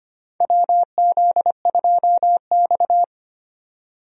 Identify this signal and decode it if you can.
Morse code, heard as a keyed tone, about 25 wpm, 700 Hz. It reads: WZ2X
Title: USA Call Signs - 4 Characters - Repeated 2x - 25wpm